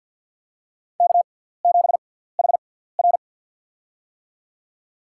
Morse code, heard as a keyed tone, about 53 wpm, spaced as for 17 wpm, 700 Hz. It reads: K7HR
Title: USA Call Signs - 4 Characters - 6x Character Spacing - 50wpm